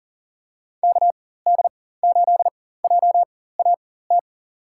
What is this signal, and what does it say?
Morse code, heard as a keyed tone, about 39 wpm, spaced as for 18 wpm, 700 Hz. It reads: KD8JAT
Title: USA Call Signs - 6 Characters - 4x Character Spacing - 40wpm